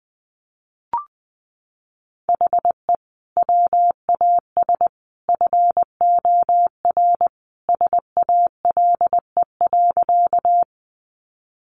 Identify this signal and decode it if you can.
Morse code, heard as a keyed tone, about 20 wpm, 700 Hz. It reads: HE WAS FOR SALE.
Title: Sentences from Top 1400 Words 20wpm